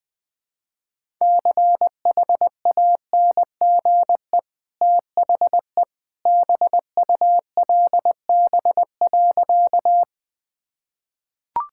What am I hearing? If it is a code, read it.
Morse code, heard as a keyed tone, about 20 wpm, 700 Hz. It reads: CHANGE THE BULB.
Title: Easy Sentences with Hints - 20wpm